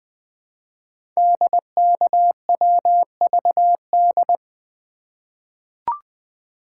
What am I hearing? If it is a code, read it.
Morse code, heard as a keyed tone, about 20 wpm, 700 Hz. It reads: DKWVD
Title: Code Groups - 5 Letters - 20wpm